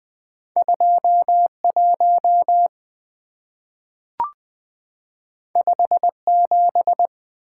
Morse code, heard as a keyed tone, about 20 wpm, 700 Hz. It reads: 2 1 5 7
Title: Two Letter-Number Combinations - 20wpm